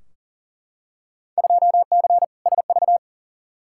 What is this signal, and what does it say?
Morse code, heard as a keyed tone, about 40 wpm, 700 Hz. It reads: 2C SV
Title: Field Day Exchanges Encore - 40wpm